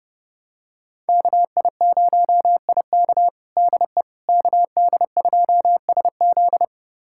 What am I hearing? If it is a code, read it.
Morse code, heard as a keyed tone, about 30 wpm, 700 Hz. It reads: KI0IK DE KD2SZ
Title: USA Callsign DE Callsign - 5 Characters - 30wpm